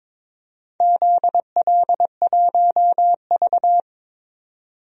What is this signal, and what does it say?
Morse code, heard as a keyed tone, about 22 wpm, 690 Hz. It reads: ZL1V
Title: DXCC-100 Call Signs - 4 Characters - 22wpm